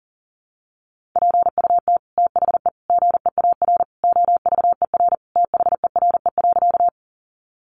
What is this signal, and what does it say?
Morse code, heard as a keyed tone, about 40 wpm, 700 Hz. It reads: PUT THE GEAR OVER THERE.
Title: Sentences from Top 1400 Words 40wpm